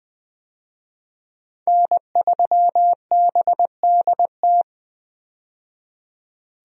Morse code, heard as a keyed tone, about 20 wpm, 700 Hz. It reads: N3BDT
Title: USA Call Signs - 5 Characters - 20wpm